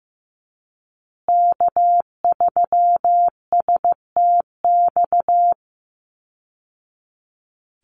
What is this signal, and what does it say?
Morse code, heard as a keyed tone, about 15 wpm, 700 Hz. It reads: K3STX
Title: Call Signs Repeated 3x - 15wpm